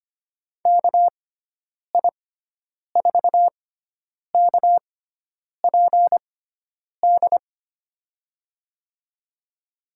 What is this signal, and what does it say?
Morse code, heard as a keyed tone, about 25 wpm, spaced as for 9 wpm, 700 Hz. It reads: KI4KPD
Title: USA Call Signs - 6 Characters - 6x Character Spacing - 25wpm